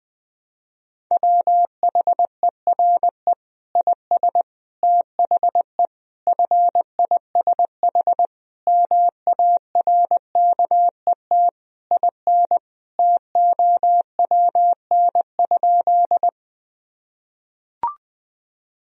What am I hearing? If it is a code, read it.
Morse code, heard as a keyed tone, about 20 wpm, 700 Hz. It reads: WHERE IS THE FISH MARKET IN TOWN?
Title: Sentences from Top 1000 Words - Review Part 3 - 20wpm